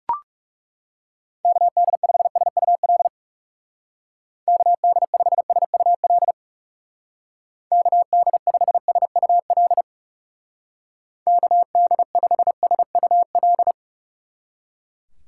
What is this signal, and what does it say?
Morse code, heard as a keyed tone, about 36 wpm, 700 Hz. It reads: KD5SUL KD5SUL KD5SUL KD5SUL
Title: Speed Racing - USA 2x3 Call Signs 30wpm